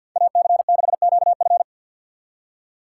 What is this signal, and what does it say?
Morse code, heard as a keyed tone, about 58 wpm, 700 Hz. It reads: AC6QF
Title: Speed Racing - USA 2x2 Call Signs 50wpm